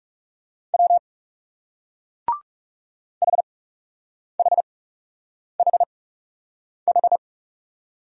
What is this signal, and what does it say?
Morse code, heard as a keyed tone, about 38 wpm, spaced as for 8 wpm, 700 Hz. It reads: W HHHH